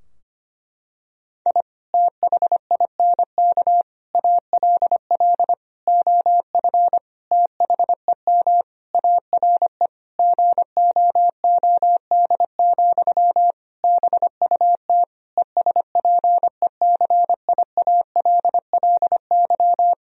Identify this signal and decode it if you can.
Morse code, heard as a keyed tone, about 25 wpm, 700 Hz. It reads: I THINK ALL OF THEM ARE GOOD, BUT ESPECIALLY
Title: Sentences from Top 1000 Words - Review Part 4 - 25wpm